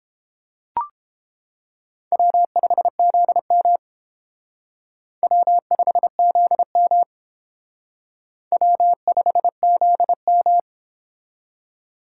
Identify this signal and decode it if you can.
Morse code, heard as a keyed tone, about 28 wpm, 700 Hz. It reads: W5ZM W5ZM W5ZM